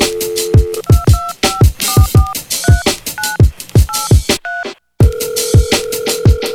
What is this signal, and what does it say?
DTMF, heard as telephone touch tones, tones of about 200 ms. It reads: 225443983
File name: music / Logic - Under-Pressure